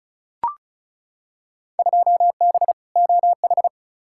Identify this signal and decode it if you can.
Morse code, heard as a keyed tone, about 34 wpm, 700 Hz. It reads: 2B OH
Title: Field Day Exchanges - 35wpm